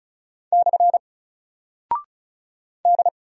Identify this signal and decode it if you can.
Morse code, heard as a keyed tone, about 36 wpm, 700 Hz. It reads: / D
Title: Single Character (Letter:Number:Punctuation:Prosign) Alt 35wpm